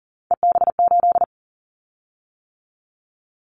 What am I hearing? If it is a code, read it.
Morse code, heard as a keyed tone, about 40 wpm, 700 Hz. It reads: EB8
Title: Callsign Prefixes for Top 100 DXCC - 40wpm